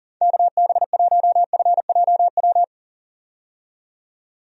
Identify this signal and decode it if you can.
Morse code, heard as a keyed tone, about 40 wpm, 700 Hz. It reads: KB1FJW
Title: USA Call Signs - 6 Characters - 40wpm